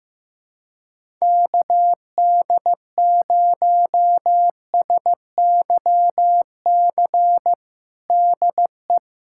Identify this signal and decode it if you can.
Morse code, heard as a keyed tone, about 15 wpm, 700 Hz. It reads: KD0SYC DE
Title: USA Callsign DE Callsign 15wpm